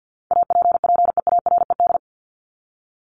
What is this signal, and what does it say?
Morse code, heard as a keyed tone, about 50 wpm, 700 Hz. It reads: APPEARED
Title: ED Suffix - Top 100 Words with Base Words 50wpm